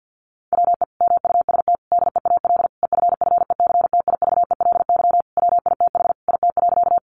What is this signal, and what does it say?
Morse code, heard as a keyed tone, about 50 wpm, 700 Hz. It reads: WE MUST DEAL EFFECTIVELY WITH IT.
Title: Sentences from Top 1400 Words 50wpm